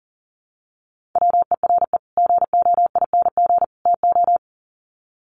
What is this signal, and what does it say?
Morse code, heard as a keyed tone, about 40 wpm, 700 Hz. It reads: WERE GOING TO
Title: Sets of 3 Words 40wpm